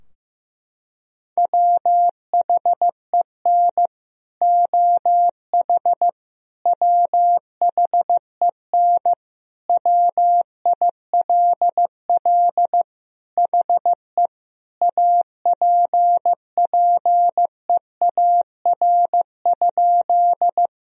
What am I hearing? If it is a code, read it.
Morse code, heard as a keyed tone, about 16 wpm, 700 Hz. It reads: WHEN OH WHEN WILL HE APPEAR?
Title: Sentences from Top 1000 Words - Review Part 2 - 15wpm